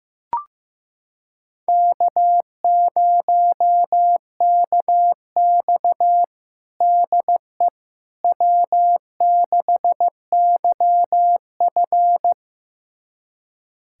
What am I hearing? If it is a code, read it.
Morse code, heard as a keyed tone, about 15 wpm, 700 Hz. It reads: K0KX DE W6YF
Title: USA Callsign DE Callsign - 4 Characters - 15wpm